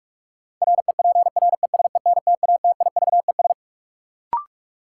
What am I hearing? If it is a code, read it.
Morse code, heard as a keyed tone, about 46 wpm, 700 Hz. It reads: REPRESENTATIVES
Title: Sets of 5 Words Encore - New Word Drill - 45wpm